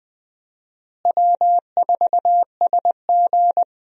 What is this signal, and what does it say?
Morse code, heard as a keyed tone, about 20 wpm, 700 Hz. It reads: W4SG
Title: USA Call Signs - 4 Characters - Repeated 2x - 20wpm